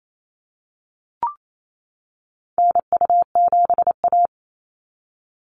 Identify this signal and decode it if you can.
Morse code, heard as a keyed tone, about 28 wpm, 700 Hz. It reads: NU7A
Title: NA SOTA Activator Callsigns 28wpm